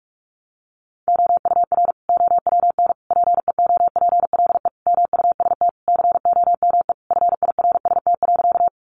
Morse code, heard as a keyed tone, about 44 wpm, 700 Hz. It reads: OUR OWN PEOPLE MUST COME FIRST.
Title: Sentences from Top 300 Words 45wpm